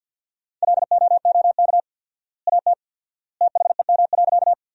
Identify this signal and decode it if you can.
Morse code, heard as a keyed tone, about 50 wpm, 700 Hz. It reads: LOOK AT THEM.
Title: 4x Word-Spacing - Sentences from Top 100 Words - 50wpm